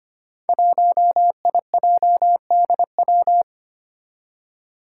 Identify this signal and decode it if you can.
Morse code, heard as a keyed tone, about 25 wpm, 700 Hz. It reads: 1IJDW
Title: Five Challenging Letter-Number Combinations 25wpm